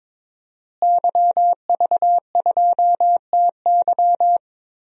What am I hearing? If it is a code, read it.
Morse code, heard as a keyed tone, about 22 wpm, 700 Hz. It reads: YV2TY
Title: DXCC-100 Call Signs - 5 Characters - Repeated 3x - 22wpm